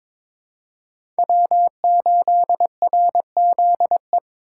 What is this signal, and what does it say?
Morse code, heard as a keyed tone, about 23 wpm, 700 Hz. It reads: W8RZE
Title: USA Call Signs - 5 Characters - 4x Character Spacing - 22wpm